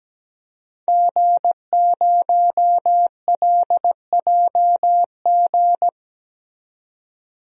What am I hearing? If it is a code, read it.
Morse code, heard as a keyed tone, about 17 wpm, 700 Hz. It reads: G0LJG
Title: DXCC-100 Call Signs - 5 Characters - 17wpm